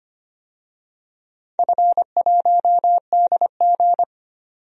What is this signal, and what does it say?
Morse code, heard as a keyed tone, about 25 wpm, 700 Hz. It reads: F1DG
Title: DXCC-100 Call Signs - 4 Characters - 25wpm